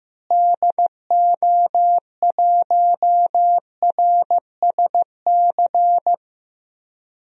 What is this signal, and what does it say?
Morse code, heard as a keyed tone, about 15 wpm, 700 Hz. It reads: DO1RSC
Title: DXCC-100 Call Signs - 6 Characters - Repeated 2x - 15wpm